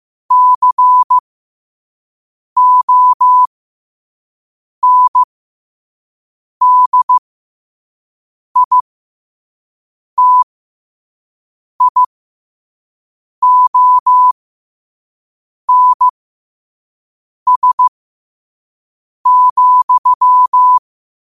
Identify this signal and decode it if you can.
Morse code, heard as a keyed tone, about 15 wpm, spaced as for 5 wpm, 1000 Hz. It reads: CONDITIONS,
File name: qotdfiles / 5wpm